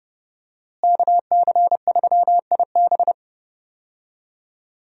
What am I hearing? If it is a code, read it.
Morse code, heard as a keyed tone, about 30 wpm, 700 Hz. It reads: KC3IB
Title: USA Call Signs - 5 Characters - Repeated 3x - 30wpm